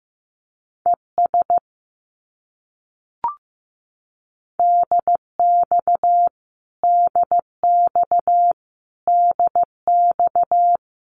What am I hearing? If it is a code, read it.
Morse code, heard as a keyed tone, about 15 wpm, 700 Hz. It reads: ES DX DX DX
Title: Abbreviations Repeated 3x - 15wpm